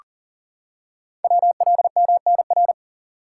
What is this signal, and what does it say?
Morse code, heard as a keyed tone, about 40 wpm, 690 Hz. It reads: WLMNR
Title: Code Groups - 5 Letters - 40wpm